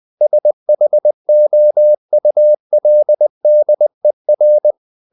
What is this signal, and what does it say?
Morse code, heard as a keyed tone, about 20 wpm, 600 Hz. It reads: SHOULDER
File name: the old man and the sea part 2 20wpm